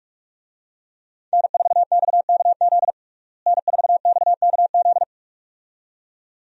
Morse code, heard as a keyed tone, about 47 wpm, 700 Hz. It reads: N4XKZ N4XKZ
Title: USA Call Signs - 5 Characters - Repeated 2x - 45wpm